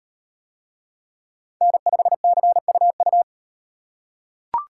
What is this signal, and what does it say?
Morse code, heard as a keyed tone, about 34 wpm, 700 Hz. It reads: N5CUU